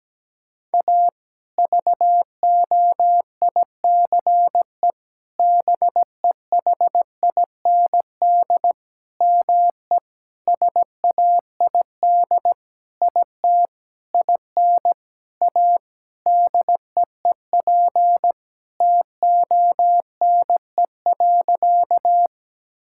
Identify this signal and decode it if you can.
Morse code, heard as a keyed tone, about 17 wpm, 700 Hz. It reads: A VOICE BEHIND ME SAID IT IN A DEEP TONE.
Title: Sentences from Top 1600 Words 17wpm